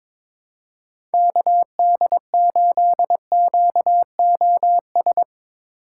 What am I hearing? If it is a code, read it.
Morse code, heard as a keyed tone, about 22 wpm, 700 Hz. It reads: KD8QOS
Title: USA Call Signs - 6 Characters - 2x Character Spacing - 22wpm